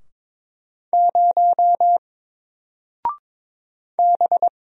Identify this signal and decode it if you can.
Morse code, heard as a keyed tone, about 22 wpm, 700 Hz. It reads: 0 B
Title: Single Letter-Number Alt 22wpm